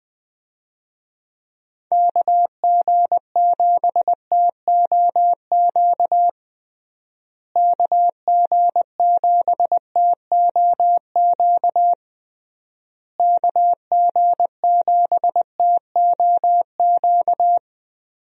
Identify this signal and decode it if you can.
Morse code, heard as a keyed tone, about 20 wpm, 700 Hz. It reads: KG7TOQ KG7TOQ KG7TOQ
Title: USA Call Signs - 6 Characters - Repeated 3x - 20wpm